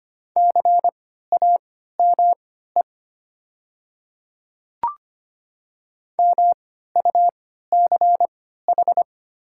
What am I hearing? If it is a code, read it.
Morse code, heard as a keyed tone, about 25 wpm, spaced as for 14 wpm, 700 Hz. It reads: CAME MUCH
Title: 3x Character Spacing - Top 200 Words - 25wpm